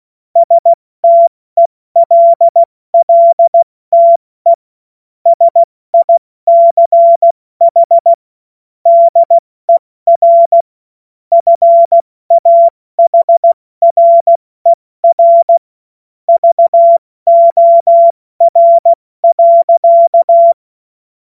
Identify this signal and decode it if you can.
Morse code, heard as a keyed tone, about 16 wpm, 680 Hz. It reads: STELLTE SICH DER FAHRER VOR.
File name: Morsekurs / Witze / Witz 03 Juni Mai 2025 70 F80